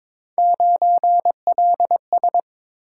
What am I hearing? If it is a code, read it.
Morse code, heard as a keyed tone, about 22 wpm, 700 Hz. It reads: 9LS
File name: practice / 22wpm / Three Letter-Number Combinations - 22wpm